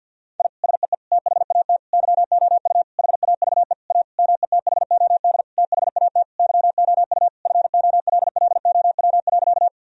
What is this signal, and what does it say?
Morse code, heard as a keyed tone, about 51 wpm, 700 Hz. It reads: I SEE THAT YOU HAVE A METHOD THAT YOU FOLLOW.
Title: Sentences from Top 1000 Words - Review Part 3 - 50wpm